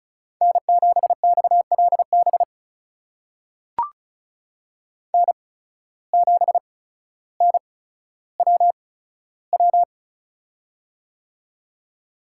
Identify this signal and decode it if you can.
Morse code, heard as a keyed tone, about 36 wpm, spaced as for 15 wpm, 700 Hz. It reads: N7XLB N7NWW
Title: USA Call Signs - 5 Characters - 8x Character Spacing - 35wpm